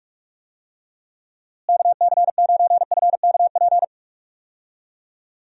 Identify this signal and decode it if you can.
Morse code, heard as a keyed tone, about 45 wpm, 700 Hz. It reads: KC9FKP
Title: USA Call Signs - 6 Characters - Repeated 2x - 45wpm